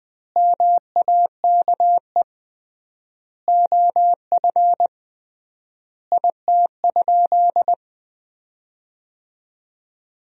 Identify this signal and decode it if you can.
Morse code, heard as a keyed tone, about 20 wpm, 700 Hz. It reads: MAKE OF IT?
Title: 3x Word-Spacing - Sentences from Top 100 Words - 20wpm